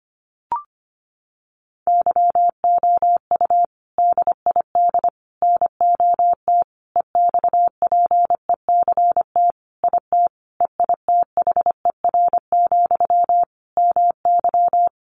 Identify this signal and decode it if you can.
Morse code, heard as a keyed tone, about 25 wpm, 700 Hz. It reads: YOU DID NOT EXPECT IT EITHER, MY
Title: Sentences from Top 1100 Words 25wpm